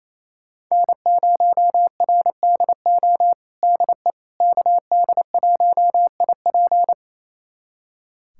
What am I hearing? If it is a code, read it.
Morse code, heard as a keyed tone, about 28 wpm, 700 Hz. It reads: N0RDO DE KD1IP
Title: USA Callsign DE Callsign - 5 Characters - 28wpm